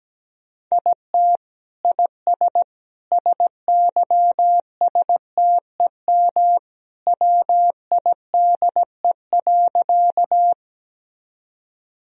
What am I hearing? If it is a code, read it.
Morse code, heard as a keyed tone, about 17 wpm, 700 Hz. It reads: IT IS SYSTEM WIDE.
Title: Sentences from Top 1000 Words - Review Part 3 - 17wpm